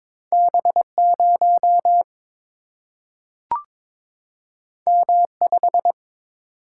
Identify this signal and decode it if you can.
Morse code, heard as a keyed tone, about 22 wpm, 700 Hz. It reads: B 0 M 5
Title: Single-Letter Single-Number 22wpm